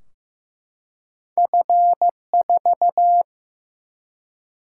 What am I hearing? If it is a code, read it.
Morse code, heard as a keyed tone, about 15 wpm, 700 Hz. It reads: F4
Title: Single-Letter Single-Number 15wpm